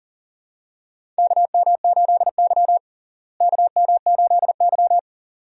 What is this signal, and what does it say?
Morse code, heard as a keyed tone, about 37 wpm, 700 Hz. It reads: KM8Y KM8Y
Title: USA Call Signs - 4 Characters - Repeated 2x - 40wpm